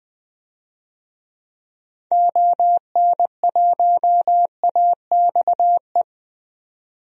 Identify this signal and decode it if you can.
Morse code, heard as a keyed tone, about 20 wpm, 700 Hz. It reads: ON1AXE
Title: DXCC-100 Call Signs - 6 Characters - 20wpm